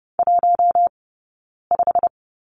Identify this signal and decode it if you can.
Morse code, heard as a keyed tone, about 30 wpm, spaced as for 9 wpm, 690 Hz. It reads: 15
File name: course / 30:10wpm / 086 Lesson 15d - Number 5 - Two Characters 30wpm